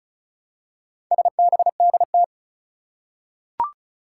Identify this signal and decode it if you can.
Morse code, heard as a keyed tone, about 36 wpm, 700 Hz. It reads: SBDT
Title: Four Challenging Letter-Number Combinations 35wpm